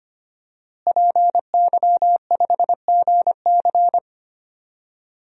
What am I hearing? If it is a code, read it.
Morse code, heard as a keyed tone, about 25 wpm, 700 Hz. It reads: PY5GC
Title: DXCC-100 Call Signs - 5 Characters - Repeated 2x - 25wpm